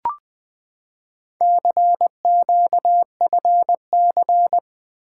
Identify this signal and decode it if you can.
Morse code, heard as a keyed tone, about 20 wpm, 700 Hz. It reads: CQFC
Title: Four Letter-Number Combinations - 20wpm